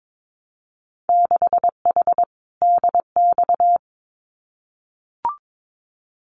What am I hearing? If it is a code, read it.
Morse code, heard as a keyed tone, about 22 wpm, 700 Hz. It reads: 6H DX
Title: Winter Field Day Exchanges Encore 22wpm